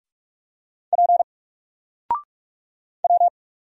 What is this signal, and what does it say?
Morse code, heard as a keyed tone, about 44 wpm, 700 Hz. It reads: P W
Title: Single Letter-Number Alt 45wpm